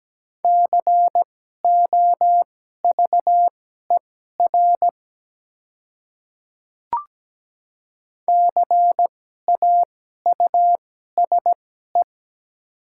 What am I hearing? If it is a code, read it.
Morse code, heard as a keyed tone, about 17 wpm, spaced as for 12 wpm, 700 Hz. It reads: COVER CAUSE
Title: 2x Character Spacing - Top 300 Words - 17wpm